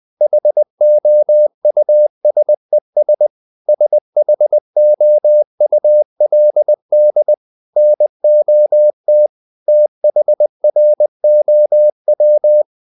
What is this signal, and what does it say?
Morse code, heard as a keyed tone, about 20 wpm, 600 Hz. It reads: HOUSES SHOULD NOT THROW
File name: CW / Practice / EnglishSayings20